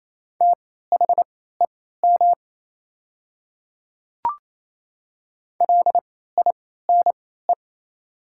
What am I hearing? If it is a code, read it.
Morse code, heard as a keyed tone, about 27 wpm, spaced as for 16 wpm, 700 Hz. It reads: THEM LINE
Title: 3x Character Spacing - Top 200 Words - 28wpm